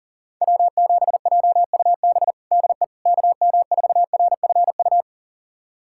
Morse code, heard as a keyed tone, about 37 wpm, 700 Hz. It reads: W7JUB DE KM4RFU
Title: USA Callsign DE Callsign 40wpm